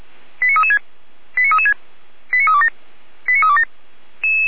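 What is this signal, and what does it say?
Five-tone selective call (ZVEI), heard as quick runs of five tones, tones of about 70 ms. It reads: 89336 89336 89327 89327
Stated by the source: ZVEI five-tone call 89325 received on an SDR.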